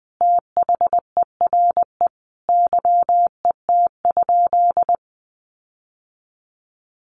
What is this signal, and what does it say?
Morse code, heard as a keyed tone, about 19 wpm, 700 Hz. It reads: THERE YET?
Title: Sentences from Top 1300 Words 20wpm